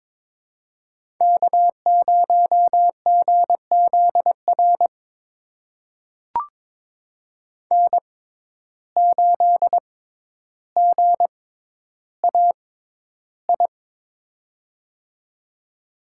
Morse code, heard as a keyed tone, about 21 wpm, spaced as for 11 wpm, 700 Hz. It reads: K0GZR N8GAI
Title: USA Call Signs - 5 Characters - 6x Character Spacing - 22wpm